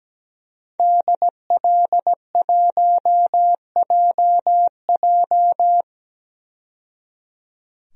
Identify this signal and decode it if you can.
Morse code, heard as a keyed tone, about 17 wpm, 700 Hz. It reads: DL1JJ
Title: DXCC-100 Call Signs - 5 Characters - 17wpm